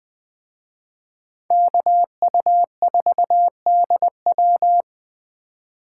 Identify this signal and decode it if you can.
Morse code, heard as a keyed tone, about 21 wpm, 700 Hz. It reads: KU4DW